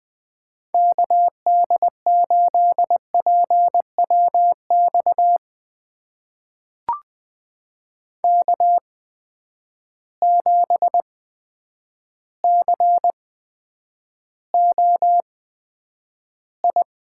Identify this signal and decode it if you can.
Morse code, heard as a keyed tone, about 20 wpm, spaced as for 9 wpm, 700 Hz. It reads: KD8PWX K7COI
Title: USA Call Signs - 8x Character Spacing - 20wpm